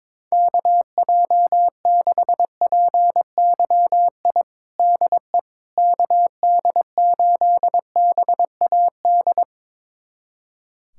Morse code, heard as a keyed tone, about 21 wpm, 700 Hz. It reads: KJ6PYI DE KD8BAD
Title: USA Callsign DE Callsign - 6 Characters - 22wpm